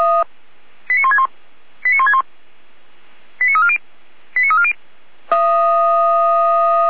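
Five-tone selective call (ZVEI), heard as quick runs of five tones, tones of about 70 ms. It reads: 89161 89161 89340 89340
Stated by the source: Five-tone ZVEI selective call received by SDR, Feuerwehr Orlamünde.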